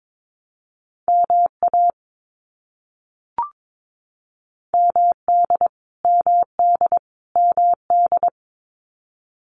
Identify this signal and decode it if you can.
Morse code, heard as a keyed tone, about 22 wpm, 700 Hz. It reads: MA MD MD MD
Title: US States Repeated 3x 22wpm